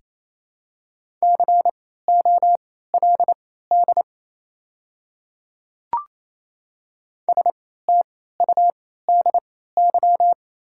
Morse code, heard as a keyed tone, about 28 wpm, spaced as for 16 wpm, 700 Hz. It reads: COLD STUDY
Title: 3x Character Spacing - Top 500 Words - 28wpm